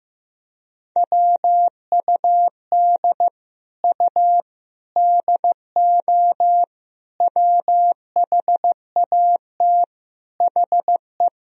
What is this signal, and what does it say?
Morse code, heard as a keyed tone, about 15 wpm, 700 Hz. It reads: WUD U DO WHAT HE
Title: Sentences from Top 100 Words and Abbreviations 15wpm